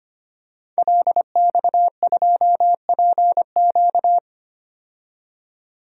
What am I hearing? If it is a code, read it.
Morse code, heard as a keyed tone, about 25 wpm, 700 Hz. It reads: LX2PQ
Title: DXCC-100 Call Signs - 5 Characters - 25wpm